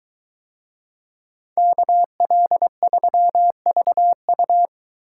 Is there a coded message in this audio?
Yes: Morse code from 1.6 to 4.7 s, heard as a keyed tone, about 22 wpm, 700 Hz. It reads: KL3VU